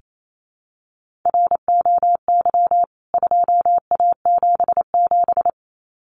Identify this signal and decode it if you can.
Morse code, heard as a keyed tone, about 28 wpm, 700 Hz. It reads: ROY 2A77
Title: CWT Exchange 28wpm